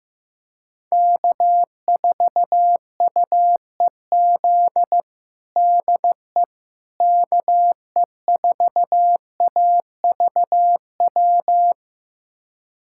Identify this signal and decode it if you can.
Morse code, heard as a keyed tone, about 15 wpm, 700 Hz. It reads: K4UEZ DE KE4AVW
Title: USA Callsign DE Callsign 15wpm